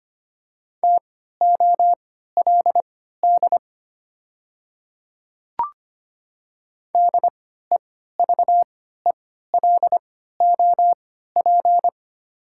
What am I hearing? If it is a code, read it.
Morse code, heard as a keyed tone, about 25 wpm, spaced as for 14 wpm, 700 Hz. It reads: TOLD DEVELOP
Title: 3x Character Spacing - Top 500 Words - 25wpm